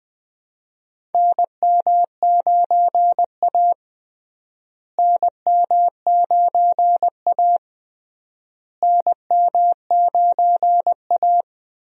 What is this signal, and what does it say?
Morse code, heard as a keyed tone, about 20 wpm, 700 Hz. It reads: NM9A NM9A NM9A
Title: USA Call Signs - 4 Characters - Repeated 3x - 20wpm